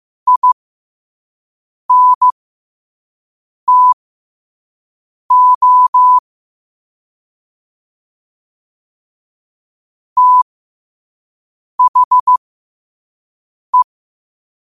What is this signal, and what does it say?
Morse code, heard as a keyed tone, about 15 wpm, spaced as for 5 wpm, 1000 Hz. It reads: INTO THE
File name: qotdfiles / 5wpm